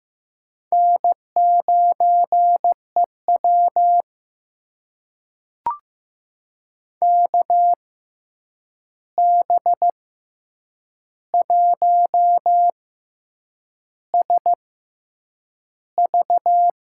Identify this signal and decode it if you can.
Morse code, heard as a keyed tone, about 15 wpm, spaced as for 7 wpm, 700 Hz. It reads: N9EW KB1SV
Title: USA Call Signs - 6x Character Spacing - 15wpm